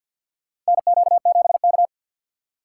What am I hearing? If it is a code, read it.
Morse code, heard as a keyed tone, about 53 wpm, 690 Hz. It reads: NQ7K
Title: USA Call Signs - 2x Character Spacing - 50wpm